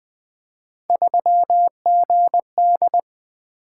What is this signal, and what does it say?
Morse code, heard as a keyed tone, about 20 wpm, 700 Hz. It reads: 3GD
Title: Three Letter-Number Combinations - 20wpm